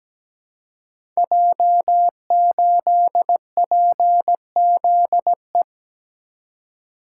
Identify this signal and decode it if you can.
Morse code, heard as a keyed tone, about 17 wpm, 700 Hz. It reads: J8PZE
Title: Code Groups - 5 Letter:Numbers - 17wpm